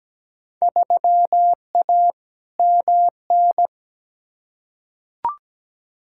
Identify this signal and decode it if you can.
Morse code, heard as a keyed tone, about 17 wpm, 700 Hz. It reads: 3A MN
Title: Field Day Exchanges - 17wpm